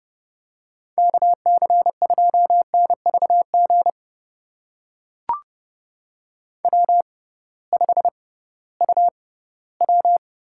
Morse code, heard as a keyed tone, about 29 wpm, spaced as for 17 wpm, 700 Hz. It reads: KC2NVG W5UW
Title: USA Call Signs - 6x Character Spacing - 30wpm